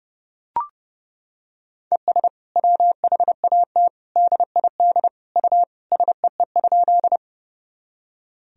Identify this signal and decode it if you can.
Morse code, heard as a keyed tone, about 30 wpm, 700 Hz. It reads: ES WHAT DID U SEE?
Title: Sentences from Top 100 Words and Abbreviations 30wpm